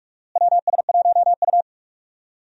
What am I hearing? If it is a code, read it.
Morse code, heard as a keyed tone, about 46 wpm, 700 Hz. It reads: WS1U
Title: USA Call Signs - 4 Characters - Repeated 2x - 45wpm